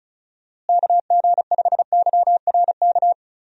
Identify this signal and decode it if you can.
Morse code, heard as a keyed tone, about 34 wpm, 700 Hz. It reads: KG5YRK
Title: USA Call Signs - 6 Characters - Repeated 3x - 35wpm